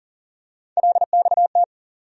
Morse code, heard as a keyed tone, about 40 wpm, 700 Hz. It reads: LXT
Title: USA Call Sign Postfixes - 3 Letters 40wpm